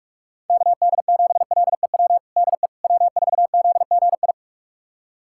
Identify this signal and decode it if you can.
Morse code, heard as a keyed tone, about 46 wpm, 700 Hz. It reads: KD7LEW DE W4ZGI
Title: USA Callsign DE Callsign 45wpm